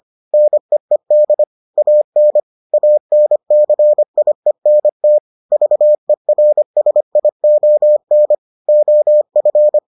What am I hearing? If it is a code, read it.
Morse code, heard as a keyed tone, about 25 wpm, 600 Hz. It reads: NEED AN ANCIENT VERSION OF